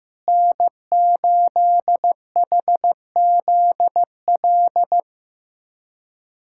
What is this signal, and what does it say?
Morse code, heard as a keyed tone, about 15 wpm, 700 Hz. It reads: N8HZL
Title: USA Callsign DE Callsign - 5 Characters - 15wpm